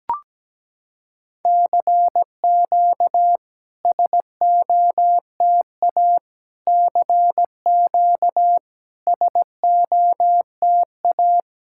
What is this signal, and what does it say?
Morse code, heard as a keyed tone, about 17 wpm, 700 Hz. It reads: CQ SOTA CQ SOTA
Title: SOTA QSO Practice 17wpm